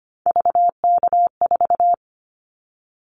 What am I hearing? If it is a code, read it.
Morse code, heard as a keyed tone, about 25 wpm, 700 Hz. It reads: VK4
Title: Callsign Prefixes for Top 50 DXCC - 25wpm